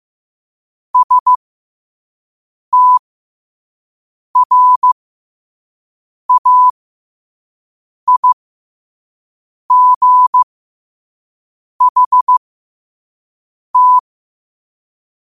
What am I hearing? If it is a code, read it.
Morse code, heard as a keyed tone, about 15 wpm, spaced as for 5 wpm, 1000 Hz. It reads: STRAIGHT
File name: qotdfiles / 5wpm